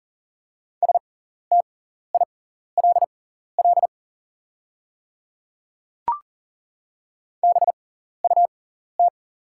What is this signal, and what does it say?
Morse code, heard as a keyed tone, about 40 wpm, spaced as for 14 wpm, 700 Hz. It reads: STILL BUT
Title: 6x Character Spacing - Top 300 Words - 40wpm